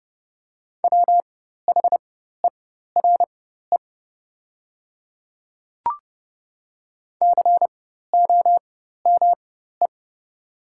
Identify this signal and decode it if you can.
Morse code, heard as a keyed tone, about 30 wpm, spaced as for 14 wpm, 690 Hz. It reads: WHERE COME
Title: 4x Character Spacing - Top 300 Words - 30wpm